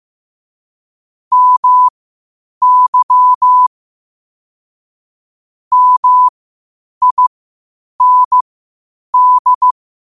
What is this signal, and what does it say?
Morse code, heard as a keyed tone, about 15 wpm, spaced as for 8 wpm, 1000 Hz. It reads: MY MIND
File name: qotdfiles / 8wpm